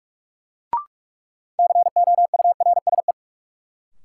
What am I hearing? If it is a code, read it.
Morse code, heard as a keyed tone, about 42 wpm, 700 Hz. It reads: COURSE
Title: Sets of 2 Words - New Word Drill - 45wpm